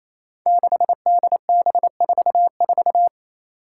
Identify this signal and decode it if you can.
Morse code, heard as a keyed tone, about 28 wpm, 700 Hz. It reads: 6DB44
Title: Five Challenging Letter-Number Combinations 28wpm